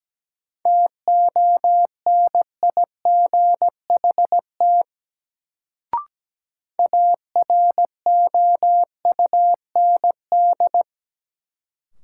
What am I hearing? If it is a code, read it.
Morse code, heard as a keyed tone, about 17 wpm, 700 Hz. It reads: TONIGHT AROUND
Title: Sentences from Top 600 Words - New Word Drill - 17wpm